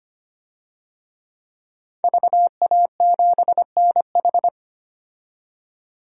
Morse code, heard as a keyed tone, about 25 wpm, 700 Hz. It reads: VA7NH
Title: DXCC-100 Call Signs - 5 Characters - 25wpm